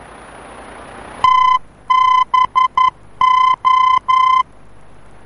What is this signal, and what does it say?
Morse code, heard as a keyed tone, about 11 wpm, 1020 Hz. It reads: TBO
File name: france / mx / TBO-F-VOR